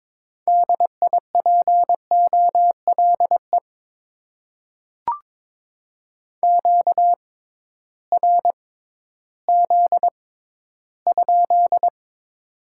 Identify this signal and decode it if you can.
Morse code, heard as a keyed tone, about 22 wpm, spaced as for 13 wpm, 700 Hz. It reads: DIPOLE QRZ?
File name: Top 90 QSO Elements - 6x Character Spacing - 22wpm